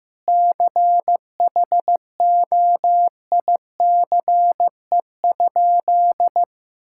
Morse code, heard as a keyed tone, about 15 wpm, 700 Hz. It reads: CHOICE?
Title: Sentences from Top 1000 Words - Review Part 6 - 15wpm